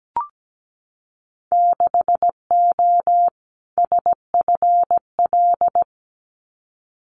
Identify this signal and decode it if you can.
Morse code, heard as a keyed tone, about 17 wpm, 700 Hz. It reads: 6O SFL
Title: Winter Field Day Exchanges Encore 17wpm